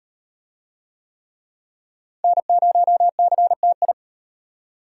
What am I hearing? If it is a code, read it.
Morse code, heard as a keyed tone, about 38 wpm, 700 Hz. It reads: N0CTI